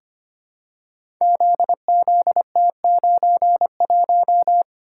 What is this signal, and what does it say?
Morse code, heard as a keyed tone, about 25 wpm, 700 Hz. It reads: ZZT91
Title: Code Groups - 5 Letter:Numbers - 25wpm